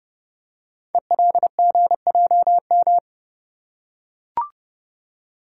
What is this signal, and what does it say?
Morse code, heard as a keyed tone, about 30 wpm, 700 Hz. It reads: ELGJM
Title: Code Groups - 5 Letters - 30wpm